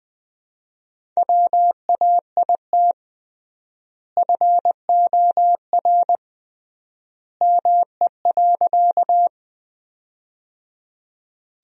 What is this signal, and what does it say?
Morse code, heard as a keyed tone, about 20 wpm, 700 Hz. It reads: WAIT FOR ME.
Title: 3x Word-Spacing - Sentences from Top 500 Words - 20wpm